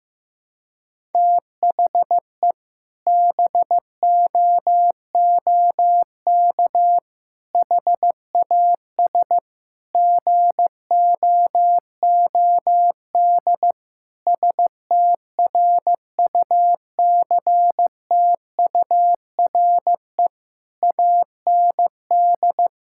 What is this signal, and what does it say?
Morse code, heard as a keyed tone, about 16 wpm, 700 Hz. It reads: THE BOOK HAS GOOD STRUCTURE AND
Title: Sentences from Top 1000 Words - Review Part 6 - 15wpm